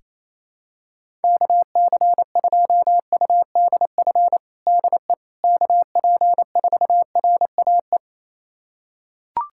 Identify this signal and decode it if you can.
Morse code, heard as a keyed tone, about 28 wpm, 700 Hz. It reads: KC2UDF DE KP4RAE
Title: USA Callsign DE Callsign - 6 Characters - 28wpm